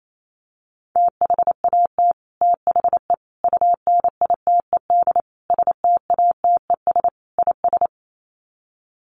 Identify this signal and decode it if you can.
Morse code, heard as a keyed tone, about 28 wpm, 700 Hz. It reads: THAT THE UNITED STATES IS
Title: Sets of 5 Words Encore 28wpm